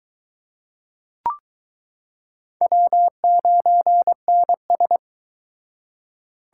Morse code, heard as a keyed tone, about 22 wpm, 700 Hz. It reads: W9NS